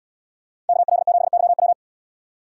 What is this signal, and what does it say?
Morse code, heard as a keyed tone, about 70 wpm, 700 Hz. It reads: KL7YW